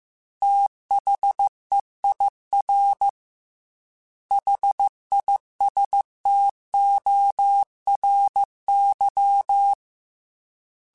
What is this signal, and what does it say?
Morse code, heard as a keyed tone, about 15 wpm, 800 Hz. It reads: THEIR HISTORY
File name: T then H Runs / T then H Runs.mp3s / T then H Runs @13